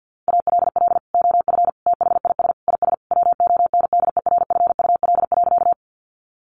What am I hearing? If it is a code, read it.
Morse code, heard as a keyed tone, about 50 wpm, 700 Hz. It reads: ALL OF THIS IS WONDERFUL.
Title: Sentences from Top 1100 Words 50wpm